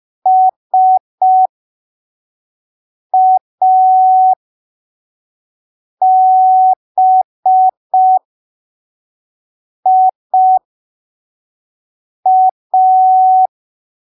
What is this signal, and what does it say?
Morse code, heard as a keyed tone, about 5 wpm, spaced as for 3 wpm, 750 Hz. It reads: SABIA
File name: arquivos / texto / texto 17 05ppm